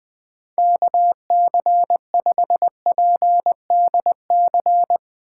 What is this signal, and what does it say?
Morse code, heard as a keyed tone, about 20 wpm, 700 Hz. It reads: KC5PDC
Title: USA Call Signs - 6 Characters - 4x Character Spacing - 20wpm